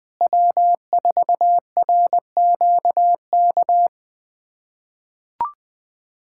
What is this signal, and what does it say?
Morse code, heard as a keyed tone, about 20 wpm, 700 Hz. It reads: W4RQK
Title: USA Callsign DE Callsign - 5 Characters - 20wpm